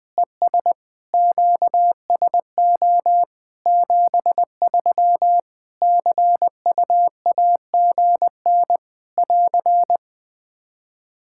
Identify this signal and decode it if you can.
Morse code, heard as a keyed tone, about 20 wpm, 700 Hz. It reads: ES QSO 73 CUAGN <AR>
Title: Practice QSO - Exchange 4 - Ending - 20wpm